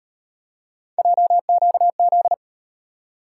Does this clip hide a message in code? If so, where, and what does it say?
Yes: Morse code from 1.0 to 2.3 s, heard as a keyed tone, about 36 wpm, 700 Hz. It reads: JQZ